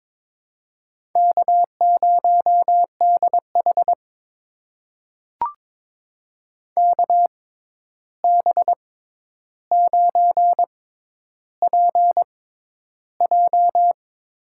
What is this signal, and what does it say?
Morse code, heard as a keyed tone, about 22 wpm, spaced as for 11 wpm, 700 Hz. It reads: K0DH KB9PJ
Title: USA Call Signs - 6x Character Spacing - 22wpm